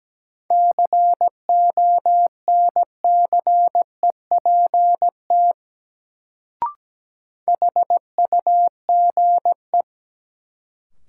Morse code, heard as a keyed tone, about 17 wpm, 700 Hz. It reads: CONCEPT HUGE
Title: Sentences from Top 1100 Words - New Word Drill Alt - 17wpm